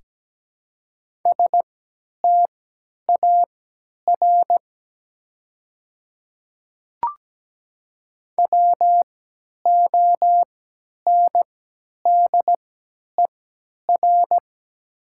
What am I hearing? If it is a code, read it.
Morse code, heard as a keyed tone, about 17 wpm, spaced as for 10 wpm, 700 Hz. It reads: STAR WONDER
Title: 3x Character Spacing - Top 500 Words - 17wpm